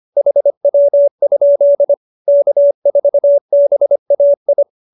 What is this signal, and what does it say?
Morse code, heard as a keyed tone, about 25 wpm, 570 Hz. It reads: HW? K4BAI
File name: QSO301 25